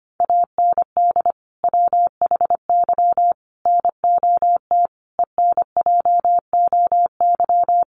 Morse code, heard as a keyed tone, about 25 wpm, 700 Hz. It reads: AND WHY NOT ENJOY
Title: Sentences from Top 1300 Words 25wpm